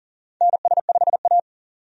Morse code, heard as a keyed tone, about 40 wpm, 700 Hz. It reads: NS5A
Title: USA Call Signs - 4 Characters - 40wpm